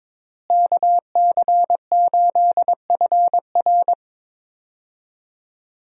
Morse code, heard as a keyed tone, about 21 wpm, 700 Hz. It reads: KC8FR
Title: USA Call Signs - 5 Characters - Repeated 2x - 22wpm